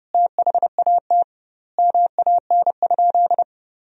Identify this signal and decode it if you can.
Morse code, heard as a keyed tone, about 32 wpm, 700 Hz. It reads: THAT MAN?
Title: 2x Word-Spacing - Sentences from Top 200 Words - 30wpm